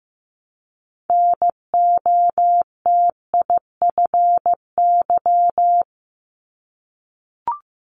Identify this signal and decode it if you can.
Morse code, heard as a keyed tone, about 15 wpm, 700 Hz. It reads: NOTIFY
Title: Multi-letter Phonemes 15wpm